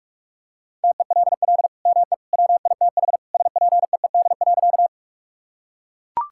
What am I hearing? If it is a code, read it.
Morse code, heard as a keyed tone, about 41 wpm, 700 Hz. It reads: TELL ME WITH SPEED.
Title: Sentences from Top 1000 Words - Review Part 5 - 45wpm